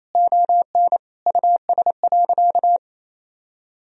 Morse code, heard as a keyed tone, about 28 wpm, 700 Hz. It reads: ON US.
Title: Sentences from Top 1000 Words - Review Part 1 - 28wpm